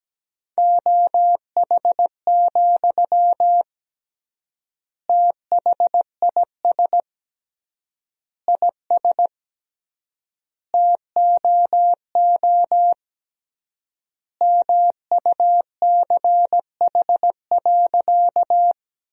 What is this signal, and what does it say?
Morse code, heard as a keyed tone, about 17 wpm, 700 Hz. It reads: OH, THIS IS TOO MUCH.
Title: 3x Word-Spacing - Sentences from Top 500 Words - 17wpm